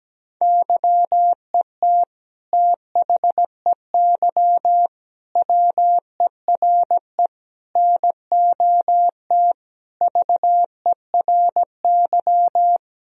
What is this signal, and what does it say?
Morse code, heard as a keyed tone, about 17 wpm, 700 Hz. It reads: YET THEY WERE NOT VERY
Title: Sentences from Top 1000 Words - Review Part 3 - 17wpm